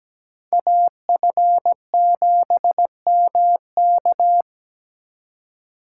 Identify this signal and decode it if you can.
Morse code, heard as a keyed tone, about 20 wpm, 700 Hz. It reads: AF7MK
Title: Speed Racing - USA Call Signs 17wpm